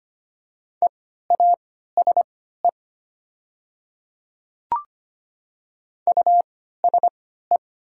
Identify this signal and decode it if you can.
Morse code, heard as a keyed tone, about 25 wpm, spaced as for 14 wpm, 700 Hz. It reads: EASE USE
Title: 3x Character Spacing - Top 300 Words - 25wpm